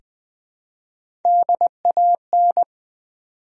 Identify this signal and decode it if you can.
Morse code, heard as a keyed tone, about 21 wpm, 700 Hz. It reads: DAN
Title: USA Common Names - Max 4 Letters - 20wpm